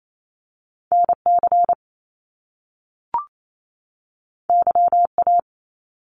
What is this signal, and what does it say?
Morse code, heard as a keyed tone, about 28 wpm, 700 Hz. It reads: NC YA
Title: Top Two Letter Combinations in Words 28wpm